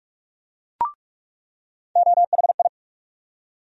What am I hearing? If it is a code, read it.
Morse code, heard as a keyed tone, about 44 wpm, 700 Hz. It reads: OHI
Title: USA Call Sign Postfixes - 3 Letters 45wpm